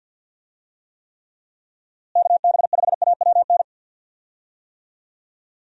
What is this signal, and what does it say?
Morse code, heard as a keyed tone, about 51 wpm, 700 Hz. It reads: KB5AWN
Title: USA Call Signs - 6 Characters - Repeated 3x - 50wpm